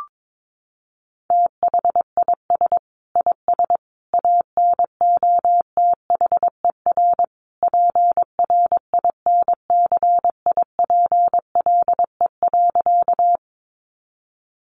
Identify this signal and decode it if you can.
Morse code, heard as a keyed tone, about 22 wpm, 700 Hz. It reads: THIS IS ANOTHER PRINCIPLE.
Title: Sentences from Top 1400 Words 22wpm